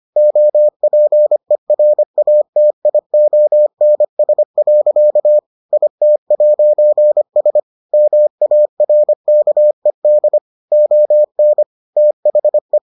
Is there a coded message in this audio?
Yes: Morse code from 0.2 to 12.8 s, heard as a keyed tone, about 25 wpm, 600 Hz. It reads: OPERATIONS. IT'S MARKED ON THE